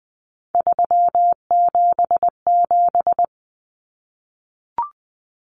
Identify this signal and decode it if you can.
Morse code, heard as a keyed tone, about 20 wpm, 700 Hz. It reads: 377
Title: Common Cut Numbers - 3-Digit 20wpm